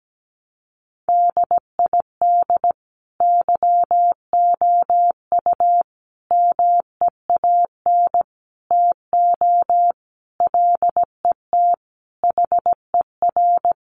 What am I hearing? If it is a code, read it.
Morse code, heard as a keyed tone, about 17 wpm, 700 Hz. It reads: DID YOU MEAN TO LET HER
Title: Sentences from Top 1300 Words 17wpm